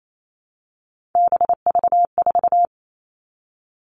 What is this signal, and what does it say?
Morse code, heard as a keyed tone, about 28 wpm, 700 Hz. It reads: BV4
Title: All Cut Numbers - 3-Digit 28wpm